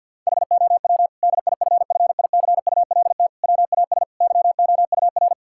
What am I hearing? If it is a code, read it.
Morse code, heard as a keyed tone, about 50 wpm, 700 Hz. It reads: HOW DIFFICULT WAS YOUR
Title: Sentences from Top 1000 Words - Review Part 6 - 50wpm